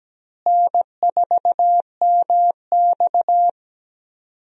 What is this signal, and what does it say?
Morse code, heard as a keyed tone, about 17 wpm, 700 Hz. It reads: N4MX
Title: USA Call Signs - 4 Characters - Repeated 3x - 17wpm